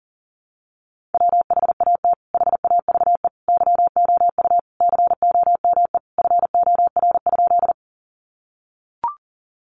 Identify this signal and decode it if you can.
Morse code, heard as a keyed tone, about 40 wpm, 700 Hz. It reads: WHAT HAVE YOU COME FOR?
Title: Sentences from Top 100 Words 40wpm